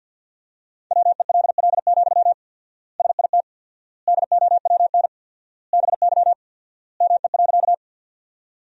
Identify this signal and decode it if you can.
Morse code, heard as a keyed tone, about 47 wpm, 700 Hz. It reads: WELL, SIT DOWN BY ME.
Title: 4x Word-Spacing - Sentences from Top 500 Words - 50wpm